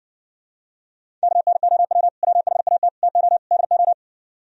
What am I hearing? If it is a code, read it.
Morse code, heard as a keyed tone, about 69 wpm, 700 Hz. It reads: KNOW WHAT TO DO